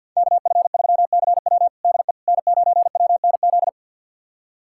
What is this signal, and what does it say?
Morse code, heard as a keyed tone, about 51 wpm, 700 Hz. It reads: KF3CW DE N9WNZ
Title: USA Callsign DE Callsign - 5 Characters - 50wpm